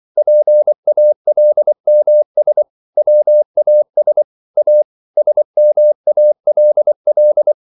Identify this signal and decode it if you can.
Morse code, heard as a keyed tone, about 24 wpm, 600 Hz. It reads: PALMS WAS A SMALL